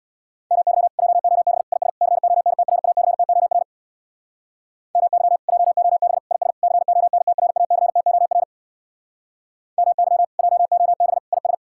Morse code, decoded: MY JOB IS CONTRACTOR MY JOB IS CONTRACTOR MY JOB IS

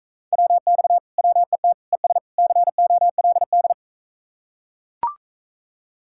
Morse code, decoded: WX WET ES COLD

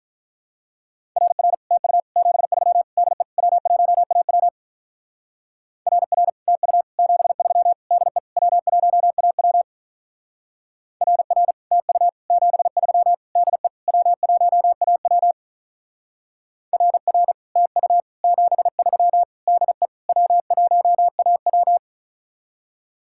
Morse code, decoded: RR TU 73 DE W1AW RR TU 73 DE W1AW RR TU 73 DE W1AW RR TU 73 DE W1AW

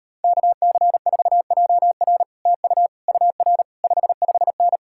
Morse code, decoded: KC4JR TU UR 55N

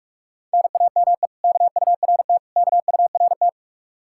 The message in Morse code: NAME KURT KURT

